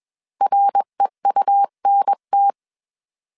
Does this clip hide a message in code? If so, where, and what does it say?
Yes: Morse code from 0.4 to 2.5 s, heard as a keyed tone, about 21 wpm, 800 Hz. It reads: REUNT